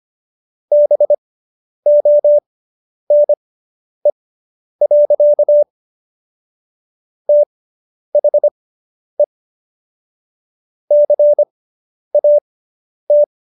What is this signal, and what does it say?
Morse code, heard as a keyed tone, about 25 wpm, spaced as for 10 wpm, 590 Hz. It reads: BONE. THE CAT